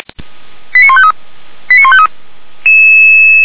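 Five-tone selective call (ZVEI), heard as quick runs of five tones, tones of about 70 ms. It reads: 89153 89153
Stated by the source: ZVEI five-tone selective call, received on SDR.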